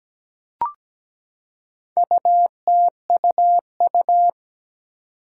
Morse code, heard as a keyed tone, about 17 wpm, 700 Hz. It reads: UTUU
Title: Four Challenging Letter-Number Combinations 17wpm